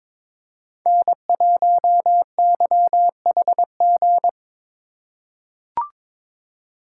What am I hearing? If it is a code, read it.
Morse code, heard as a keyed tone, about 22 wpm, 700 Hz. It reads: N1YHG
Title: USA Call Signs - 5 Characters - 3x Character Spacing - 22wpm